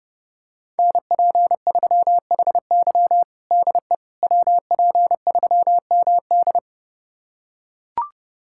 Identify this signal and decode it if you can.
Morse code, heard as a keyed tone, about 30 wpm, 700 Hz. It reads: NP3HY DE WP3MD
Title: USA Callsign DE Callsign - 5 Characters - 30wpm